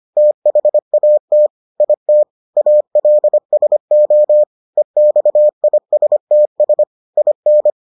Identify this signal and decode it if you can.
Morse code, heard as a keyed tone, about 25 wpm, 600 Hz. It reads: THAT IT ALSO EXISTS IN